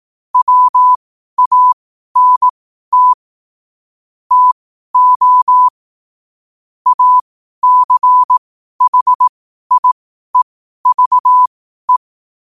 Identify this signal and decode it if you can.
Morse code, heard as a keyed tone, about 18 wpm, spaced as for 12 wpm, 1000 Hz. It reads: WANT TO ACHIEVE
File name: qotdfiles / 12wpm